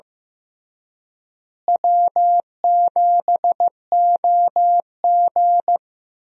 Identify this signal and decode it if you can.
Morse code, heard as a keyed tone, about 15 wpm, 700 Hz. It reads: W7OG
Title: USA Call Signs - 4 Characters - Repeated 3x - 15wpm